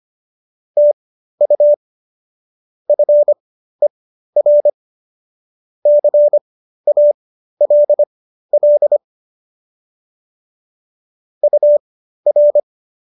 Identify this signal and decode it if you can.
Morse code, heard as a keyed tone, about 25 wpm, spaced as for 13 wpm, 590 Hz. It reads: TU FER CALL UR